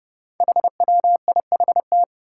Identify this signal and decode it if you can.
Morse code, heard as a keyed tone, about 30 wpm, 700 Hz. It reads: HWIHT
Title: Five Challenging Letter-Number Combinations 30wpm